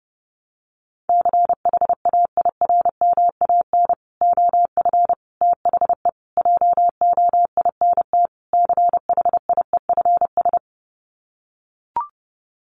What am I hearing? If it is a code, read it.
Morse code, heard as a keyed tone, about 30 wpm, 700 Hz. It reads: CHAIRMAN OF THE JOINT CHIEFS
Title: Sets of 5 Words Encore 30wpm